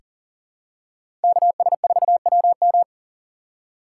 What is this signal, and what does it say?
Morse code, heard as a keyed tone, about 40 wpm, 700 Hz. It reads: KS4WM
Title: USA Call Signs - 5 Characters - 3x Character Spacing - 40wpm